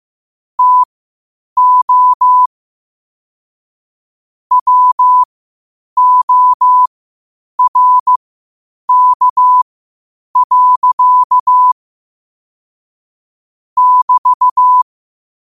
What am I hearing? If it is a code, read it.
Morse code, heard as a keyed tone, about 15 wpm, spaced as for 8 wpm, 1000 Hz. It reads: TO WORK. <BT>
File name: qotdfiles / 8wpm